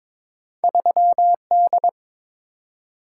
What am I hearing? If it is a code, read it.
Morse code, heard as a keyed tone, about 22 wpm, 700 Hz. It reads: 3D
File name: practice / 22wpm / Two Letter-Number Combinations - 22wpm